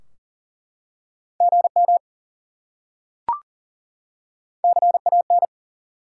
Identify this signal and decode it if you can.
Morse code, heard as a keyed tone, about 40 wpm, 700 Hz. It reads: GM CAN
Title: Top 90 QSO Elements 40wpm